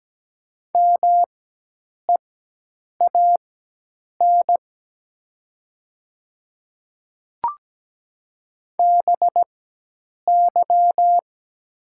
Morse code, heard as a keyed tone, about 17 wpm, spaced as for 8 wpm, 700 Hz. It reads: MEAN BY
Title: 4x Character Spacing - Top 200 Words - 17wpm